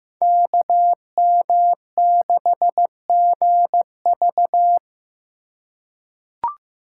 Morse code, heard as a keyed tone, about 15 wpm, 700 Hz. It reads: KM6GV
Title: USA Call Signs - 5 Characters - 4x Character Spacing - 15wpm